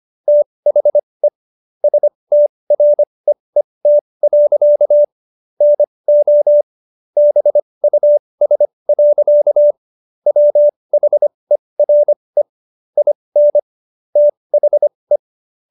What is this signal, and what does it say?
Morse code, heard as a keyed tone, about 25 wpm, spaced as for 20 wpm, 590 Hz. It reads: THE STREET. NO BUS. WHERE IN THE